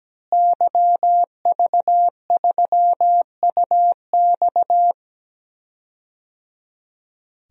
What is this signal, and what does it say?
Morse code, heard as a keyed tone, about 17 wpm, 700 Hz. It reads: YV3UX
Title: DXCC-100 Call Signs - 5 Characters - Repeated 3x - 17wpm